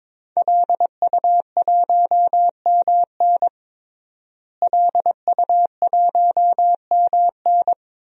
Morse code, heard as a keyed tone, about 21 wpm, 700 Hz. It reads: LU1MN LU1MN
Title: DXCC-100 Call Signs - 5 Characters - Repeated 2x - 22wpm